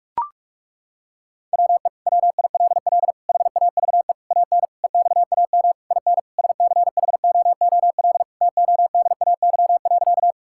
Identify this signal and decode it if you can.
Morse code, heard as a keyed tone, about 41 wpm, 700 Hz. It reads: WE WILL HAVE AN EXAM IN SCHOOL TODAY.
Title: Sentences from Top 1000 Words - Review Part 5 - 45wpm